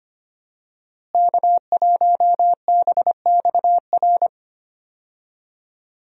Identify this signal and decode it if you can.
Morse code, heard as a keyed tone, about 25 wpm, 700 Hz. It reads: K1BXR
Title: USA Call Signs - 5 Characters - Repeated 3x - 25wpm